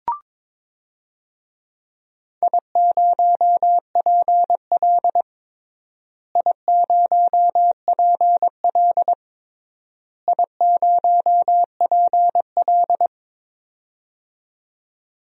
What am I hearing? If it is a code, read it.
Morse code, heard as a keyed tone, about 22 wpm, 700 Hz. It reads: I0PL I0PL I0PL
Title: DXCC-100 Call Signs - 4 Characters - Repeated 3x - 22wpm